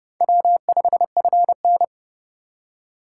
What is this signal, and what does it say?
Morse code, heard as a keyed tone, about 37 wpm, 700 Hz. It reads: W5FN
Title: Speed Racing - USA 1x2 Call Signs 30wpm